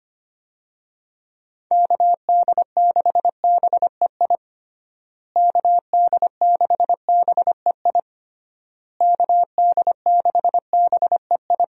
Morse code, heard as a keyed tone, about 25 wpm, 700 Hz. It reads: KD6BEI KD6BEI KD6BEI
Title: USA Call Signs - 6 Characters - Repeated 3x - 25wpm